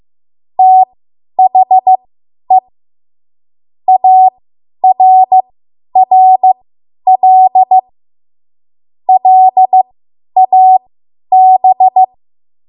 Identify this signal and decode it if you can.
Morse code, heard as a keyed tone, about 15 wpm, spaced as for 10 wpm, 750 Hz. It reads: THE ARRL LAB
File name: Morse / 10WPM